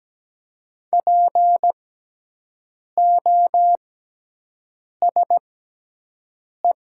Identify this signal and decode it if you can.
Morse code, heard as a keyed tone, about 17 wpm, spaced as for 6 wpm, 700 Hz. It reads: POSE
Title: 6x Character Spacing - Top 500 Words - 17wpm